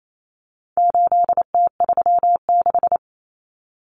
Morse code, heard as a keyed tone, about 28 wpm, 700 Hz. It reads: 8T36
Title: Common Cut Numbers - 4-Digit 28wpm